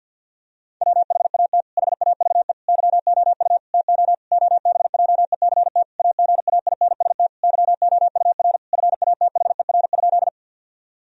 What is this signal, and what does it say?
Morse code, heard as a keyed tone, about 52 wpm, 700 Hz. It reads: WHAT HAVE YOU TO OBJECT AGAINST YOUR FATHER?
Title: Sentences from Top 1000 Words - Review Part 3 - 50wpm